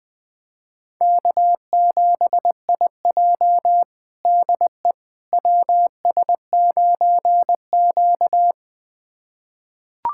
Morse code, heard as a keyed tone, about 20 wpm, 700 Hz. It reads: K7IJ DE WS9Q
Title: USA Callsign DE Callsign - 4 Characters - 20wpm